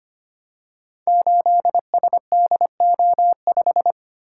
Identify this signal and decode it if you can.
Morse code, heard as a keyed tone, about 25 wpm, 700 Hz. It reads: 8SDO5
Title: Code Groups - 5 Letter:Numbers - 25wpm